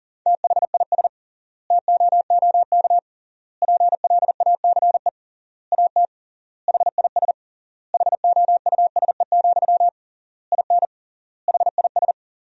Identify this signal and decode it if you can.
Morse code, heard as a keyed tone, about 40 wpm, 700 Hz. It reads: THIS TOOK PLACE AT HIS HOUSE, IN HIS
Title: 3x Word-Spacing - Sentences from Top 300 Words - 40wpm